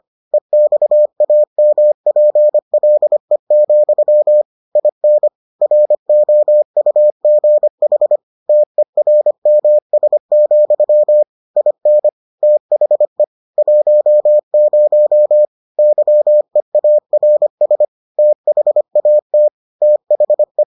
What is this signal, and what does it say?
Morse code, heard as a keyed tone, about 25 wpm, 600 Hz. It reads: EXAMPLE, IN ROUGH TERMS, IN THE 10 YEARS THAT THE